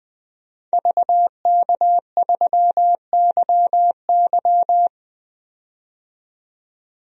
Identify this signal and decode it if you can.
Morse code, heard as a keyed tone, about 20 wpm, 700 Hz. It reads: VK3YY
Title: DXCC-100 Call Signs - 5 Characters - 20wpm